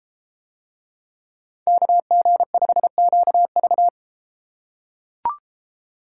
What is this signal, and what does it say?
Morse code, heard as a keyed tone, about 30 wpm, 700 Hz. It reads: KG5QV